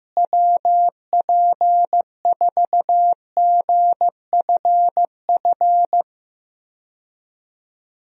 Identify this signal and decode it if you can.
Morse code, heard as a keyed tone, about 15 wpm, 700 Hz. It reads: WP4GFF
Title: USA Call Signs - 6 Characters - Repeated 3x - 15wpm